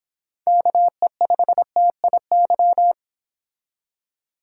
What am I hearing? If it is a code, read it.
Morse code, heard as a keyed tone, about 29 wpm, 700 Hz. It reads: KE5TIY